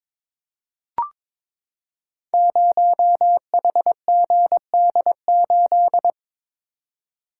Morse code, heard as a keyed tone, about 22 wpm, 700 Hz. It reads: 0HGD8
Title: Code Groups - 5 Letter:Numbers - 22wpm